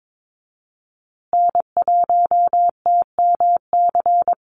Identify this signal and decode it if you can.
Morse code, heard as a keyed tone, about 22 wpm, 700 Hz. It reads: N1TMC
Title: USA Call Signs Repeated 2x 22wpm